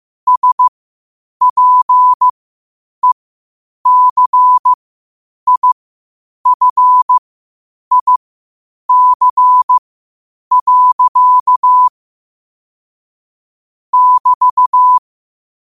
Morse code, heard as a keyed tone, about 15 wpm, spaced as for 8 wpm, 1000 Hz. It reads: SPECIFIC. <BT>
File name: qotdfiles / 8wpm